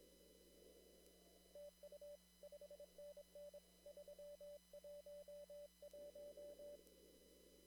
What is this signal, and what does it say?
Morse code, heard as a keyed tone, about 25 wpm, 580 Hz. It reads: TU 5NN 311
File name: CW / kt5j wpx cw-op-k5pi